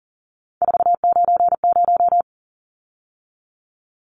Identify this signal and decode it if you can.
Morse code, heard as a keyed tone, about 40 wpm, 700 Hz. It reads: 490
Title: Numbers - 3-Digit 40wpm